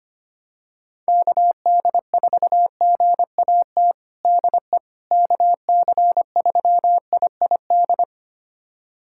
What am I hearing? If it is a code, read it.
Morse code, heard as a keyed tone, about 25 wpm, 700 Hz. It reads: KD4GAT DE KC3IID
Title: USA Callsign DE Callsign - 6 Characters - 25wpm